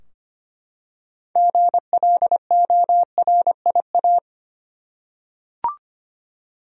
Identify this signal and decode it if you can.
Morse code, heard as a keyed tone, about 25 wpm, 700 Hz. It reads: GLORIA
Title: USA Common Names - 25wpm